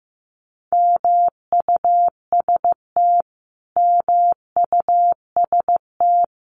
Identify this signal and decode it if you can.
Morse code, heard as a keyed tone, about 14 wpm, 700 Hz. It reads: MUST MUST
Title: Top 300 Words Repeated 15wpm